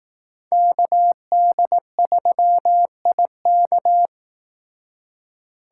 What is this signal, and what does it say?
Morse code, heard as a keyed tone, about 18 wpm, 700 Hz. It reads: KD3IK
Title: Speed Racing - USA 2x2 Call Signs 15wpm